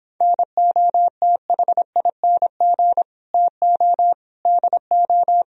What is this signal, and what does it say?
Morse code, heard as a keyed tone, about 30 wpm, 700 Hz. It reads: NOTHING TO DO